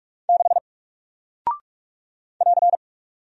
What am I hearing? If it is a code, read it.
Morse code, heard as a keyed tone, about 44 wpm, 700 Hz. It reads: 6 <AR>
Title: Single Character (Letter:Number:Punctuation:Prosign) Alt 45wpm